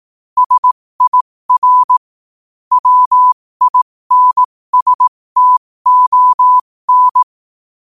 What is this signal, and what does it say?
Morse code, heard as a keyed tone, about 18 wpm, spaced as for 15 wpm, 1000 Hz. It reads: SIR WINSTON
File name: qotdfiles / 15wpm